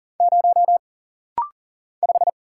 Morse code, heard as a keyed tone, about 40 wpm, 700 Hz. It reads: 0 5
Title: Single Letter-Number - Rapid-Fire 40wpm